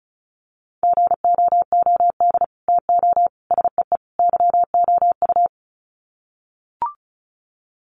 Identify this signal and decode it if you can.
Morse code, heard as a keyed tone, about 34 wpm, 700 Hz. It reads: GOOD TO SEE YOU
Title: Sets of 4 Words 35wpm